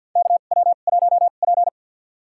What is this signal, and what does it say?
Morse code, heard as a keyed tone, about 48 wpm, spaced as for 35 wpm, 700 Hz. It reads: KW1P
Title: USA Call Signs - 2x Character Spacing - 50wpm